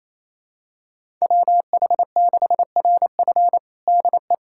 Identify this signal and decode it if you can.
Morse code, heard as a keyed tone, about 27 wpm, 700 Hz. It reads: WH6RF DE
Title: USA Callsign DE Callsign - 5 Characters - 28wpm